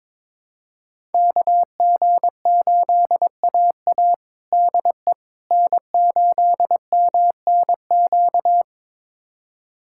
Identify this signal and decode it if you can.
Morse code, heard as a keyed tone, about 22 wpm, 700 Hz. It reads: KG8AA DE N8MNQ
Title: USA Callsign DE Callsign - 5 Characters - 22wpm